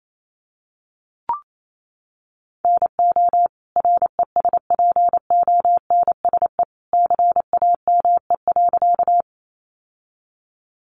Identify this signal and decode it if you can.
Morse code, heard as a keyed tone, about 28 wpm, 700 Hz. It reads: NO RESPONSE CAME.
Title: Sentences from Top 1400 Words 28wpm